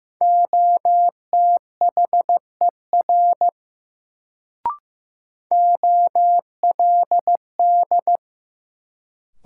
Morse code, heard as a keyed tone, about 15 wpm, 700 Hz. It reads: OTHER OLD
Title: Sets of 2 Words - New Word Drill - 15wpm